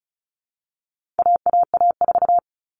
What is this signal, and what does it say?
Morse code, heard as a keyed tone, about 34 wpm, 700 Hz. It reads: AAA4
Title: All Cut Numbers - 4-Digit 35wpm